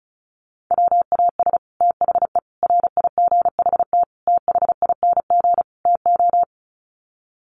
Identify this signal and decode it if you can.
Morse code, heard as a keyed tone, about 34 wpm, 700 Hz. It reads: WAS THE RIGHT THING TO
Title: Sets of 5 Words Encore2 35wpm